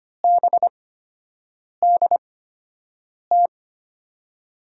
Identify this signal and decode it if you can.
Morse code, heard as a keyed tone, about 25 wpm, spaced as for 7 wpm, 700 Hz. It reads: BDT
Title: USA Call Signs - 5 Characters - 8x Character Spacing - 25wpm